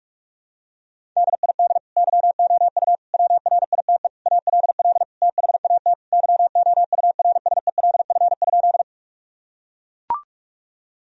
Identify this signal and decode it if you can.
Morse code, heard as a keyed tone, about 42 wpm, 700 Hz. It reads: DID YOU WRITE ALL THAT YOURSELF?
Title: Sentences from Top 1000 Words - Review Part 1 - 45wpm